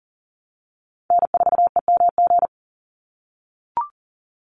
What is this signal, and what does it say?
Morse code, heard as a keyed tone, about 40 wpm, 700 Hz. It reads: N4EMG
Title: NA SOTA Chaser Callsigns 40wpm